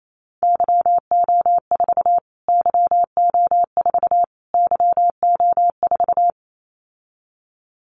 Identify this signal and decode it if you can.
Morse code, heard as a keyed tone, about 28 wpm, 700 Hz. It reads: YO4 YO4 YO4
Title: Callsign Prefixes for Top 25 DXCC - Repeated 3x - 28wpm